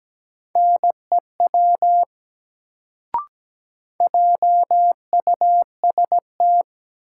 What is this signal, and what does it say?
Morse code, heard as a keyed tone, about 17 wpm, 700 Hz. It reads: NEW JUST
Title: Sets of 2 Words - New Word Drill - 17wpm